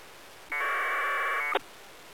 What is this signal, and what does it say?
Packet radio (AX.25, as in APRS).